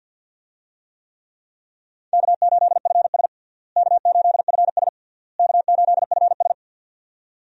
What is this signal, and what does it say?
Morse code, heard as a keyed tone, about 53 wpm, 700 Hz. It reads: K8FS K8FS K8FS
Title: USA Call Signs - 4 Characters - Repeated 3x - 50wpm